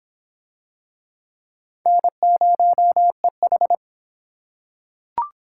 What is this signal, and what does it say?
Morse code, heard as a keyed tone, about 25 wpm, 700 Hz. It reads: N0EH